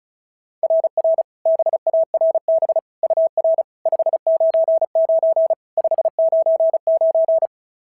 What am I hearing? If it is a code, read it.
Morse code, heard as a keyed tone, about 36 wpm, 690 Hz. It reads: RR BARB UR 599 599
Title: SOTA QSO Practice 35wpm